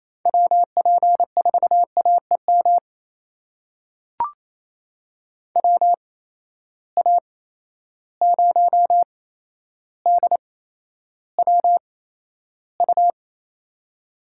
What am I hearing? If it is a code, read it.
Morse code, heard as a keyed tone, about 28 wpm, spaced as for 12 wpm, 700 Hz. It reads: WP4AEM WA0DWU
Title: USA Call Signs - 6 Characters - 8x Character Spacing - 28wpm